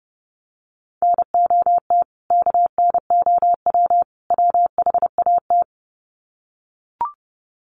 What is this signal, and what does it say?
Morse code, heard as a keyed tone, about 30 wpm, 700 Hz. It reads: NOT KNOW WHAT
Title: Sets of 3 Words 30wpm